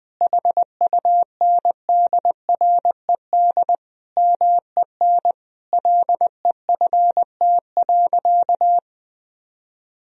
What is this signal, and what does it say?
Morse code, heard as a keyed tone, about 20 wpm, 700 Hz. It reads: HUNDRED MEN LEFT.
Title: Sentences from Top 1000 Words - Review Part 2 - 20wpm